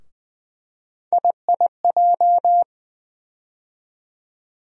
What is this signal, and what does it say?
Morse code, heard as a keyed tone, about 20 wpm, 700 Hz. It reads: IIJ
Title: USA Call Sign Postfixes - 3 Letters 20wpm